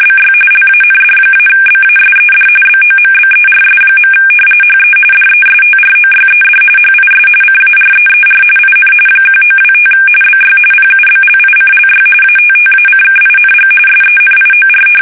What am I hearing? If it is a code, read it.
RTTY, about 45.45 baud. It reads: BY LAND-BASED BOMBERS. HE DESTROYED HIS CAREER IN 1925 BY PUBLICLY ATTACKING SENIOR
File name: fsk850